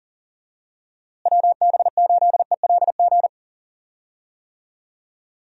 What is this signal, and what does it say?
Morse code, heard as a keyed tone, about 40 wpm, 700 Hz. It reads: WB8ELG
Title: USA Call Signs - 6 Characters - Repeated 2x - 40wpm